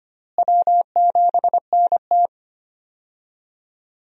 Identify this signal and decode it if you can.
Morse code, heard as a keyed tone, about 25 wpm, 700 Hz. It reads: W7NT
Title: USA Call Signs - 4 Characters - Repeated 3x - 25wpm